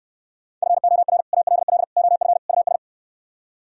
Morse code, heard as a keyed tone, about 69 wpm, 700 Hz. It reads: FOR ALL OF US